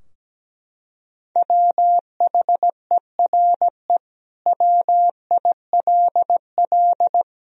Morse code, heard as a keyed tone, about 17 wpm, 700 Hz. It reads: WHERE WILL
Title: Sentences from Top 1000 Words - Review Part 2 - 17wpm